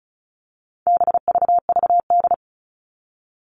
Morse code, heard as a keyed tone, about 34 wpm, 700 Hz. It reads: BVVD
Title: All Cut Numbers - 4-Digit 35wpm